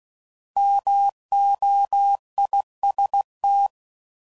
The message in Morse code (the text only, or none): MOIST